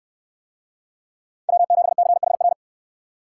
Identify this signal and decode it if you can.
Morse code, heard as a keyed tone, about 67 wpm, 700 Hz. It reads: W7CHR